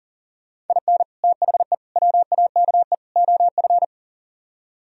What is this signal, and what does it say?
Morse code, heard as a keyed tone, about 51 wpm, 700 Hz. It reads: IN THE WAKE OF